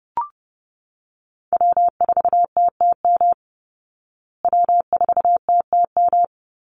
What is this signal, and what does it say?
Morse code, heard as a keyed tone, about 28 wpm, 700 Hz. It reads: W4TTM W4TTM
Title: Call Signs Repeated 3x - 30wpm